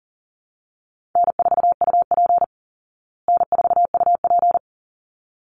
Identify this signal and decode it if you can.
Morse code, heard as a keyed tone, about 40 wpm, 700 Hz. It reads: N4UP N4UP
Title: Call Signs Repeated 3x - 40wpm